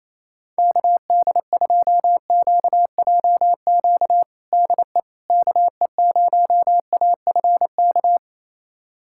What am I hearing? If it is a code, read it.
Morse code, heard as a keyed tone, about 27 wpm, 700 Hz. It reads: KD2QJQ DE KE0AFK
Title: USA Callsign DE Callsign 28wpm